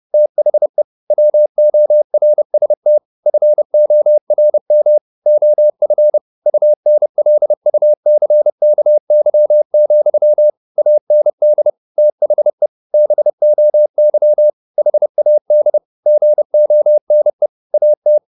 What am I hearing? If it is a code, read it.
Morse code, heard as a keyed tone, about 30 wpm, 600 Hz. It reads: THE WORST FORM OF UNLUCKY, AND THE BOY HAD GONE AT
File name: the old man and the sea / the old man and the sea 1 30wpm